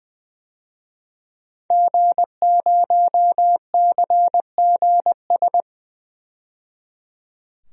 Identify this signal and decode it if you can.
Morse code, heard as a keyed tone, about 20 wpm, 700 Hz. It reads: G0CGS
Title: DXCC-100 Call Signs - 5 Characters - 20wpm